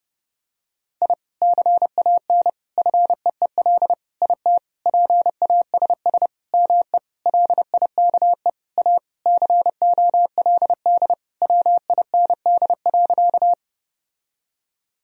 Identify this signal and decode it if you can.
Morse code, heard as a keyed tone, about 30 wpm, 700 Hz. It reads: I CAN FEEL IT PASS ME LIKE A COLD WIND.
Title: Sentences from Top 1000 Words - Review Part 2 - 30wpm